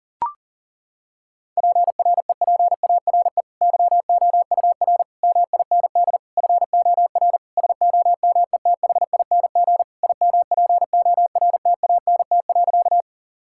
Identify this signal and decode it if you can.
Morse code, heard as a keyed tone, about 40 wpm, 700 Hz. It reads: PREPARE YOUR MIND FOR SOMETHING IMPORTANT.
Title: Sentences from Top 1000 Words - Review Part 6 - 40wpm